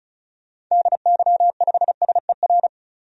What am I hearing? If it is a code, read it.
Morse code, heard as a keyed tone, about 34 wpm, 700 Hz. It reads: DY5SER
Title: DXCC-100 Call Signs - 6 Characters - 35wpm